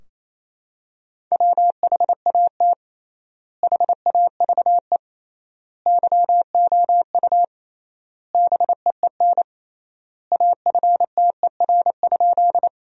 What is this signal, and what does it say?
Morse code, heard as a keyed tone, about 28 wpm, 700 Hz. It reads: WHAT HAVE YOU BEEN AFTER?
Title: 3x Word-Spacing - Sentences from Top 200 Words - 28wpm